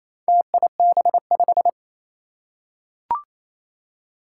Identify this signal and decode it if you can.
Morse code, heard as a keyed tone, about 28 wpm, 700 Hz. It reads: TIB5
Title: Four Challenging Letter-Number Combinations 28wpm